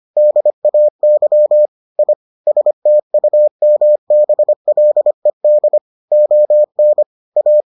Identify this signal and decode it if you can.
Morse code, heard as a keyed tone, about 25 wpm, 600 Hz. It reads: DAY I STUMBLED ON A